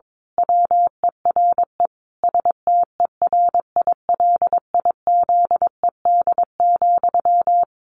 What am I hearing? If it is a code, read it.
Morse code, heard as a keyed tone, about 22 wpm, 700 Hz. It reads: WERE STERILIZED,
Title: A Sound Of Thunder 22wpm